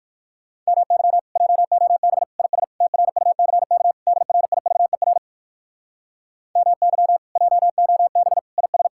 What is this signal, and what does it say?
Morse code, heard as a keyed tone, about 62 wpm, 700 Hz. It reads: MY JOB IS TRUCK DRIVER MY JOB IS